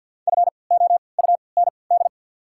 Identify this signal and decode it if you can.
Morse code, heard as a keyed tone, about 52 wpm, spaced as for 29 wpm, 690 Hz. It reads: FOUND
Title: 3x Character Spacing - Top 300 Words - 50wpm